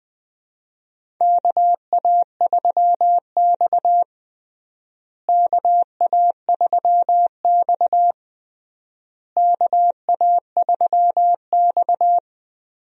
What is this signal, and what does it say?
Morse code, heard as a keyed tone, about 20 wpm, 700 Hz. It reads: KA3X KA3X KA3X
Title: USA Call Signs - 4 Characters - Repeated 3x - 20wpm